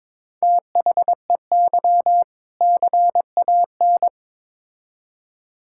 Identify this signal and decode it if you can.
Morse code, heard as a keyed tone, about 22 wpm, 700 Hz. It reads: THEY CAN
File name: Build A Phrase - 2-Word Phrases - 22wpm